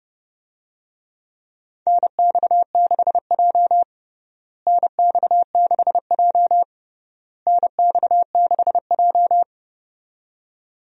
Morse code, heard as a keyed tone, about 30 wpm, 700 Hz. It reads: NX6J NX6J NX6J
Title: USA Call Signs - 4 Characters - Repeated 3x - 30wpm